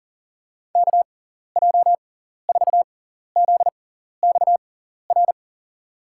Morse code, heard as a keyed tone, about 38 wpm, spaced as for 14 wpm, 700 Hz. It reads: KJ4ZXR
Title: USA Call Signs - 6x Character Spacing - 40wpm